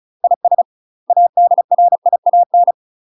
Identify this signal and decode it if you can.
Morse code, heard as a keyed tone, about 36 wpm, 700 Hz. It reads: IS ADRIAN
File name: UK MASTER QSOs TRAINING / 35WPM / UK MASTER QSOs 0024